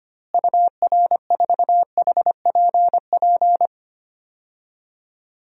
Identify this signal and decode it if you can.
Morse code, heard as a keyed tone, about 25 wpm, 700 Hz. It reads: UR4HPP
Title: DXCC-100 Call Signs - 6 Characters - Repeated 3x - 25wpm